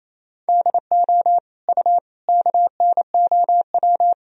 Morse code, heard as a keyed tone, about 28 wpm, 700 Hz. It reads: DO U KNOW
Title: Sentences from Top 100 Words and Abbreviations 28wpm